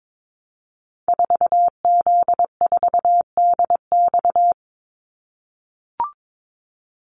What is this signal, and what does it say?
Morse code, heard as a keyed tone, about 22 wpm, 700 Hz. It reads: 4Z4DX
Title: Call Signs - 22wpm